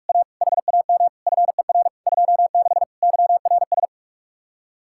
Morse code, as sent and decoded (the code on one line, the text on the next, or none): .- / .... .- -- / ..-. .-. / ..--- -.... / -.-- .-. ...
A HAM FER 26 YRS